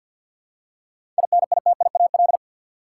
Morse code, heard as a keyed tone, about 50 wpm, 700 Hz. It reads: INITIAL
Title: Seven Letter Words - 50wpm